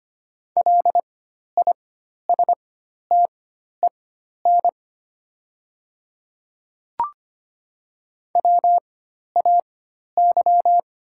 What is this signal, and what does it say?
Morse code, heard as a keyed tone, about 25 wpm, spaced as for 12 wpm, 700 Hz. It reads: LISTEN WAY
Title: 4x Character Spacing - Top 500 Words - 25wpm